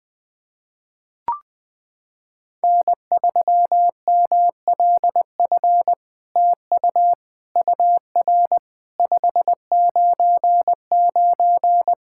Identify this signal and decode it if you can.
Morse code, heard as a keyed tone, about 20 wpm, 700 Hz. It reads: N3MLF TU UR 599
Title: SOTA QSO Practice 20wpm